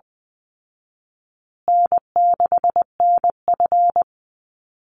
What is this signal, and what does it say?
Morse code, heard as a keyed tone, about 20 wpm, 700 Hz. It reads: N6NF
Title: Call Signs Repeated 2x - 20wpm